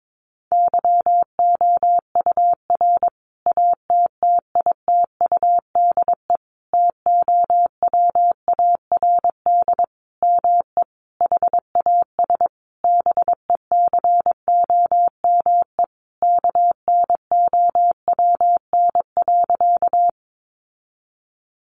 Morse code, heard as a keyed tone, about 22 wpm, 700 Hz. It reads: YOUR ATTITUDE TOWARD ME HAS BECOME KNOWN.
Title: Sentences from Top 1300 Words 22wpm